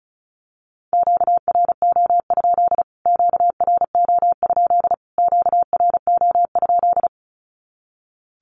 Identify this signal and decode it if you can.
Morse code, heard as a keyed tone, about 34 wpm, 700 Hz. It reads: QRO? QRO? QRO?
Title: Q-Codes Repeated 3x 35wpm